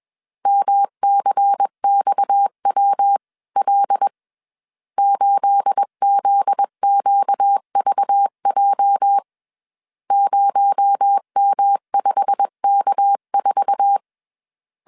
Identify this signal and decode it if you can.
Morse code, heard as a keyed tone, about 21 wpm, 800 Hz. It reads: MCXWL 8ZQVJ 0M5K4